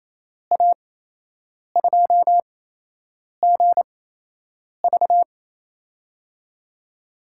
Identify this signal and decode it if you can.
Morse code, heard as a keyed tone, about 28 wpm, spaced as for 8 wpm, 700 Hz. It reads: A2GV
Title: USA Call Signs - 5 Characters - 8x Character Spacing - 28wpm